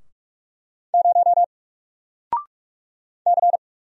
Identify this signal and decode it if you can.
Morse code, heard as a keyed tone, about 44 wpm, 700 Hz. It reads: 0 C
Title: Single Letter-Number Alt 45wpm